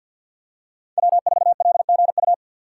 Morse code, heard as a keyed tone, about 47 wpm, 700 Hz. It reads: W4LGU
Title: Code Groups - 5 Letter:Numbers - 50wpm